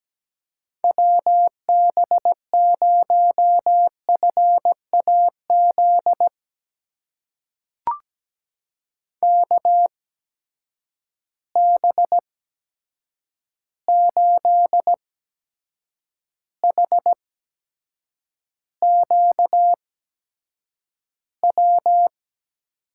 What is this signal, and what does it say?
Morse code, heard as a keyed tone, about 17 wpm, spaced as for 7 wpm, 700 Hz. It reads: WB0FAZ KB8HQW
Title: USA Call Signs - 6 Characters - 8x Character Spacing - 17wpm